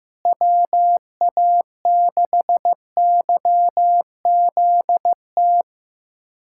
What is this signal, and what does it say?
Morse code, heard as a keyed tone, about 15 wpm, 700 Hz. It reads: WA6YZT
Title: USA Callsign DE Callsign - 6 Characters - 15wpm